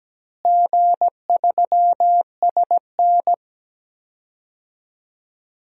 Morse code, heard as a keyed tone, about 17 wpm, 700 Hz. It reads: G3SN
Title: DXCC-100 Call Signs - 4 Characters - Repeated 2x - 17wpm